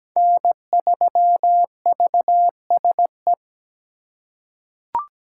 Morse code, heard as a keyed tone, about 17 wpm, 700 Hz. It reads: N3VSE
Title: USA Call Signs - 5 Characters - 2x Character Spacing - 17wpm